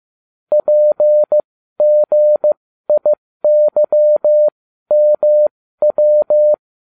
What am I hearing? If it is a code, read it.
Morse code, heard as a keyed tone, about 15 wpm, spaced as for 12 wpm, 600 Hz. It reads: PGIYMW